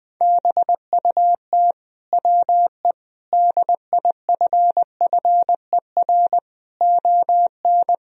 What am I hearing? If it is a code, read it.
Morse code, heard as a keyed tone, about 20 wpm, 700 Hz. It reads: BUT WE DIFFER ON
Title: Sentences from Top 1000 Words - Review Part 1 - 20wpm